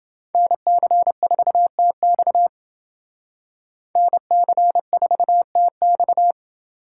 Morse code, heard as a keyed tone, about 28 wpm, 700 Hz. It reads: NC4TX NC4TX